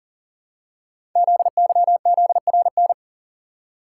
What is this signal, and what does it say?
Morse code, heard as a keyed tone, about 42 wpm, 700 Hz. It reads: ZYZRN
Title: Code Groups - 5 Letters - 40wpm